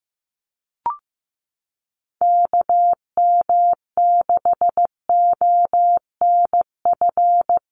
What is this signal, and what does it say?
Morse code, heard as a keyed tone, about 15 wpm, 700 Hz. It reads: KM6ONF
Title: USA Call Signs 15wpm